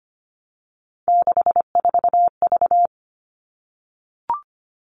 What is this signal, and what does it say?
Morse code, heard as a keyed tone, about 25 wpm, 700 Hz. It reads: 64V
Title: All Cut Numbers - 3-Digit 25wpm